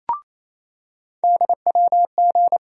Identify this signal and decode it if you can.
Morse code, heard as a keyed tone, about 28 wpm, 700 Hz. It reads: DWG
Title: USA Call Sign Postfixes - 3 Letters 28wpm